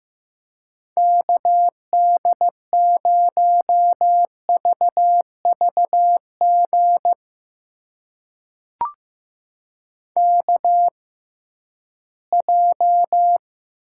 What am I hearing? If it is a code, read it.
Morse code, heard as a keyed tone, about 15 wpm, spaced as for 11 wpm, 700 Hz. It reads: KD0VVG KJ
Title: USA Call Signs - 6 Characters - 6x Character Spacing - 15wpm